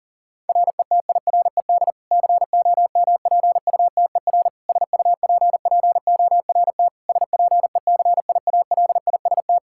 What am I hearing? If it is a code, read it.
Morse code, heard as a keyed tone, about 37 wpm, 700 Hz. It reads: RETIRED COMPUTER SUPPORT SPECIALIST
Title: Practice QSO - Exchange 3 - More Info - 40wpm